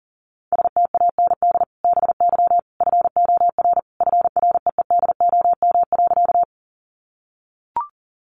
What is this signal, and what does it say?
Morse code, heard as a keyed tone, about 40 wpm, 700 Hz. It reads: STAND BY FOR FREEDOM.
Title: Sentences from Top 800 Words 40wpm